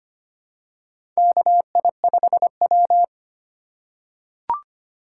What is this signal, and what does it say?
Morse code, heard as a keyed tone, about 25 wpm, 700 Hz. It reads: KI5W
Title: USA Call Signs - 4 Characters - 2x Character Spacing - 25wpm